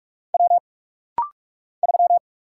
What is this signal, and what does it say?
Morse code, heard as a keyed tone, about 44 wpm, 700 Hz. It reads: W 3
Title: Single Letter-Number Alt - Rapid-Fire 45wpm